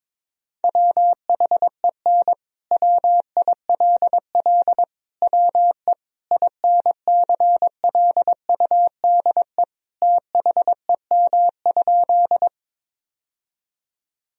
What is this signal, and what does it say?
Morse code, heard as a keyed tone, about 22 wpm, 700 Hz. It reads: WHEN WILL WE INCLUDE THEM?
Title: Sentences from Top 1000 Words - Review Part 3 - 22wpm